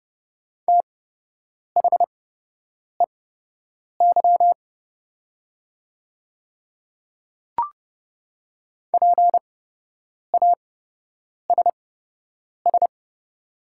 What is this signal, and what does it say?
Morse code, heard as a keyed tone, about 30 wpm, spaced as for 8 wpm, 700 Hz. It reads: THEY PASS
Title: 8x Character Spacing - Top 500 Words - 30wpm